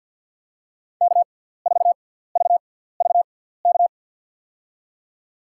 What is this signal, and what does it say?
Morse code, heard as a keyed tone, about 47 wpm, spaced as for 17 wpm, 700 Hz. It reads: K4VVK
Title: USA Call Signs - 5 Characters - 6x Character Spacing - 50wpm